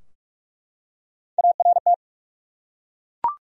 Morse code, heard as a keyed tone, about 42 wpm, 700 Hz. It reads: ART
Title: USA Common Names - Max 3 Letters - 45wpm